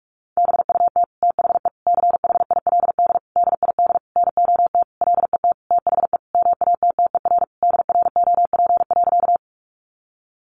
Morse code, heard as a keyed tone, about 44 wpm, 700 Hz. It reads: BUT THE CHILD DID NOT LET THE MATTER DROP.
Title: Sentences from Top 1400 Words 45wpm